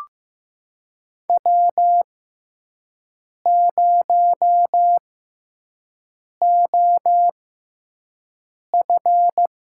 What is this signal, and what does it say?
Morse code, heard as a keyed tone, about 16 wpm, spaced as for 5 wpm, 700 Hz. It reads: W0OF
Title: USA Call Signs - 6x Character Spacing - 15wpm